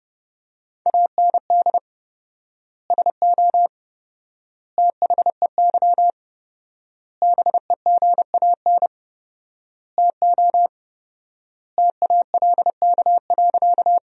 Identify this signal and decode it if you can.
Morse code, heard as a keyed tone, about 30 wpm, 700 Hz. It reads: AND SO THEY BEGAN TO TALK.
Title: 4x Word-Spacing - Sentences from Top 400 Words - 30wpm